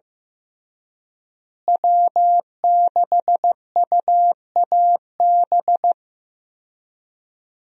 Morse code, heard as a keyed tone, about 15 wpm, 700 Hz. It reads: W6UAB
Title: USA Call Signs - 5 Characters - Repeated 2x - 15wpm